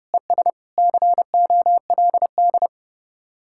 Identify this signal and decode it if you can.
Morse code, heard as a keyed tone, about 37 wpm, 700 Hz. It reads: ES COLD
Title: Speed-Racing Common QSO Phrases - 30wpm